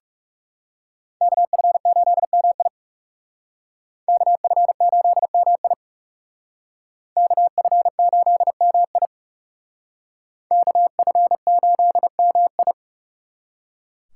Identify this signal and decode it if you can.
Morse code, heard as a keyed tone, about 38 wpm, 700 Hz. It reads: KF8MI KF8MI KF8MI KF8MI